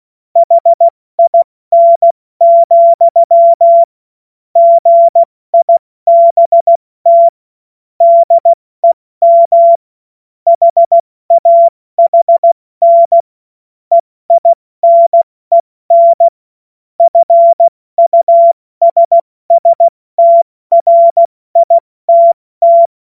Morse code, heard as a keyed tone, about 16 wpm, spaced as for 14 wpm, 680 Hz. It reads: HIN, GIBT DEM HAHN EINEN FUSSTRITT
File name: Witz-08-Juli-2025-70-F80